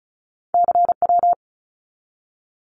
Morse code, heard as a keyed tone, about 34 wpm, 700 Hz. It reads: CW
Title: Abbreviations 35wpm